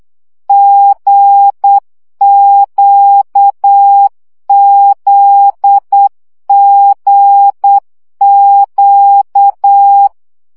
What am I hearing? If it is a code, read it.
Morse code, heard as a keyed tone, about 8 wpm, 800 Hz. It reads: GQZGQ
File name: CW gyakorlas g z q 7 baud